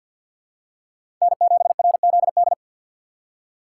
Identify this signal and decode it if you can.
Morse code, heard as a keyed tone, about 54 wpm, 700 Hz. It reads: N7RZD
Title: USA Call Signs - 6x Character Spacing - 50wpm